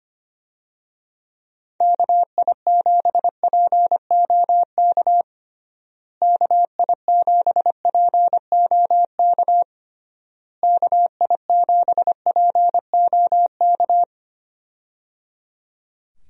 Morse code, heard as a keyed tone, about 25 wpm, 700 Hz. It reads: KI7POK KI7POK KI7POK
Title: USA Call Signs - 6 Characters - Repeated 3x - 25wpm